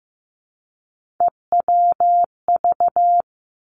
Morse code, heard as a keyed tone, about 15 wpm, 700 Hz. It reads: EWV
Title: Three Challenging Letter-Number Combinations 15wpm